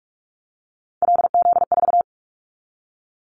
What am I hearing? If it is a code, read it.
Morse code, heard as a keyed tone, about 44 wpm, 700 Hz. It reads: LZ4
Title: Callsign Prefixes for Top 100 DXCC - 45wpm